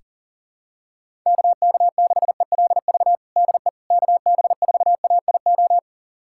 Morse code, heard as a keyed tone, about 37 wpm, 700 Hz. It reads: KK6ELV DE KB4AIO
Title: USA Callsign DE Callsign - 6 Characters - 40wpm